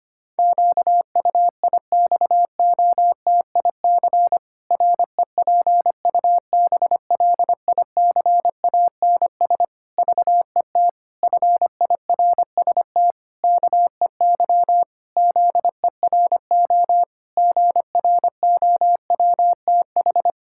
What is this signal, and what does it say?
Morse code, heard as a keyed tone, about 25 wpm, 700 Hz. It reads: QUIXOTIC REPUBLICANS VET FIRST KEY ZERO GROWTH
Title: Pangram Challenge v2 - 25wpm